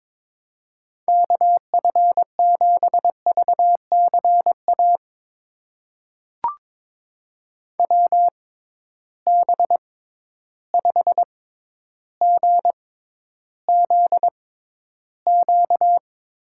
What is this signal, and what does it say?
Morse code, heard as a keyed tone, about 22 wpm, spaced as for 11 wpm, 700 Hz. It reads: KF7VCA WB5GZQ
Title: USA Call Signs - 6 Characters - 6x Character Spacing - 22wpm